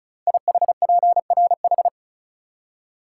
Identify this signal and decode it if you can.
Morse code, heard as a keyed tone, about 36 wpm, 700 Hz. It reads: IHPRH
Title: Code Groups - 5 Letters - 35wpm